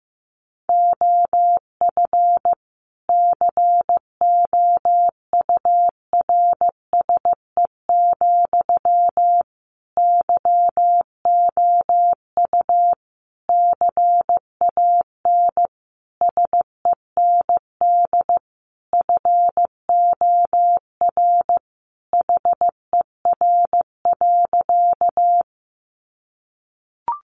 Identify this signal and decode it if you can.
Morse code, heard as a keyed tone, about 15 wpm, 700 Hz. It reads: OF COURSE, YOU CAN SEND FOR HER.
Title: Sentences from Top 1500 Words 15wpm